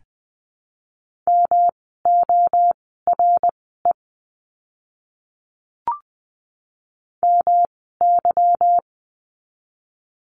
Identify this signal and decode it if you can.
Morse code, heard as a keyed tone, about 20 wpm, spaced as for 14 wpm, 700 Hz. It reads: MORE MY
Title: 206 Lesson 33f - Letter j - Words 20wpm